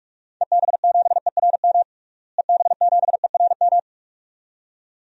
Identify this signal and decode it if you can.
Morse code, heard as a keyed tone, about 48 wpm, 700 Hz. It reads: EB7ERM EB7ERM
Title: DXCC-100 Call Signs - 6 Characters - Repeated 2x - 45wpm